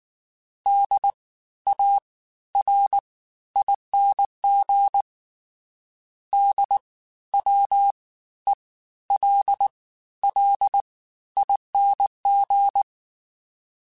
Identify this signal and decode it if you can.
Morse code, heard as a keyed tone, about 19 wpm, spaced as for 13 wpm, 800 Hz. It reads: DARING DWELLING
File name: Special INGs-mp3s / Special INGs-mp3s / Fast ing @18